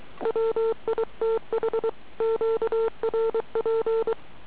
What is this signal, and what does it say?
Morse code, heard as a keyed tone, about 23 wpm, 450 Hz. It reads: WITH QRP